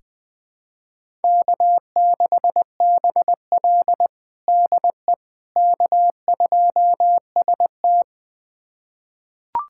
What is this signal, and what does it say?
Morse code, heard as a keyed tone, about 20 wpm, 700 Hz. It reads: K6BL DE K2ST
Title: USA Callsign DE Callsign - 4 Characters - 20wpm